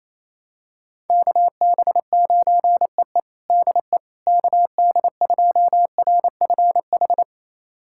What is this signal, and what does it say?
Morse code, heard as a keyed tone, about 28 wpm, 700 Hz. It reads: KB9EE DE KD2RFH
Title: USA Callsign DE Callsign 28wpm